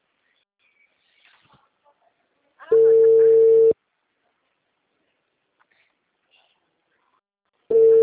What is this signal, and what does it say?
Ringback tone (China, 450 Hz), with speech.
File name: out- -2009